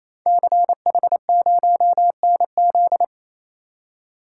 Morse code, heard as a keyed tone, about 27 wpm, 700 Hz. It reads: CH0NZ
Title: Code Groups - 5 Letter:Numbers - 28wpm